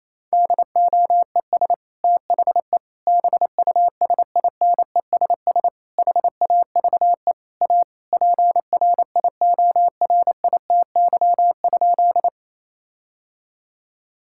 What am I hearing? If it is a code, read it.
Morse code, heard as a keyed tone, about 28 wpm, 700 Hz. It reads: DOES THE BUSINESS HAVE A PRIORITY?
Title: Sentences from Top 1600 Words 28wpm